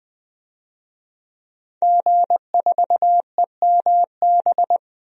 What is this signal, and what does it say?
Morse code, heard as a keyed tone, about 21 wpm, 700 Hz. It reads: G4EMB
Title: DXCC-100 Call Signs - 5 Characters - 20wpm